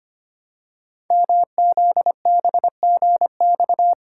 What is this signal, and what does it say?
Morse code, heard as a keyed tone, about 25 wpm, 700 Hz. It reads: MZBGX
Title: Code Groups - 5 Letters - 25wpm